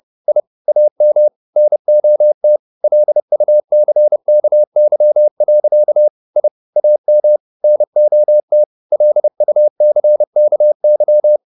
Morse code, heard as a keyed tone, about 30 wpm, 600 Hz. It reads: I AM NOT LUCKY. I AM NOT LUCKY